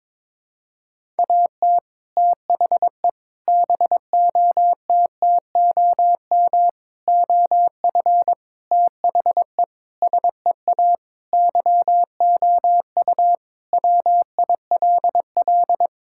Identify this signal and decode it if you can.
Morse code, heard as a keyed tone, about 22 wpm, 700 Hz. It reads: AT THE BOTTOM OF THE SEA YOU WILL
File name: practice / 22wpm / Sentences with 3-Word Phrases - 22wpm